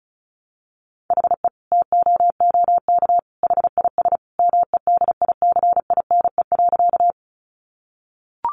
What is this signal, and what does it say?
Morse code, heard as a keyed tone, about 34 wpm, 700 Hz. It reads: HE TOOK HIS MEDICINE.
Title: Sentences from Top 900 Words 35wpm